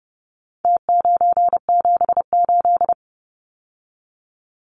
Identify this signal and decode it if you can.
Morse code, heard as a keyed tone, about 30 wpm, 700 Hz. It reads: T978
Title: Numbers - Contest Style - 4-Digit 30wpm